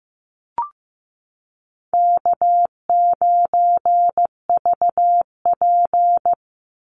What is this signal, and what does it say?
Morse code, heard as a keyed tone, about 15 wpm, 700 Hz. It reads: K9VP
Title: Easy Call Signs - 15wpm